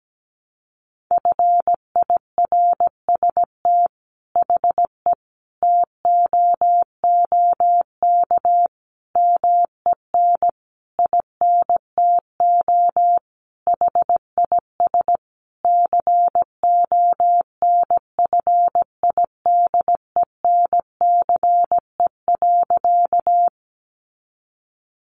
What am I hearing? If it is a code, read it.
Morse code, heard as a keyed tone, about 17 wpm, 700 Hz. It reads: FIRST HE TOOK MEN INTO HIS CONFIDENCE.
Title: Sentences from Top 1500 Words 17wpm